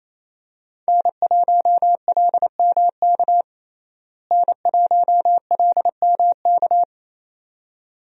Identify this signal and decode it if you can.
Morse code, heard as a keyed tone, about 28 wpm, 700 Hz. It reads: N1LMK N1LMK
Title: USA Call Signs - 5 Characters - Repeated 2x - 28wpm